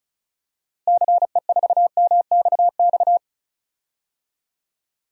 Morse code, heard as a keyed tone, about 36 wpm, 700 Hz. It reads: CE4MXX
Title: DXCC-100 Call Signs - 6 Characters - Repeated 2x - 35wpm